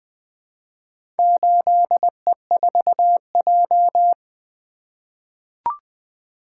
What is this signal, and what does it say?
Morse code, heard as a keyed tone, about 20 wpm, 700 Hz. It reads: 8E4J
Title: Four Letter-Number Combinations - 20wpm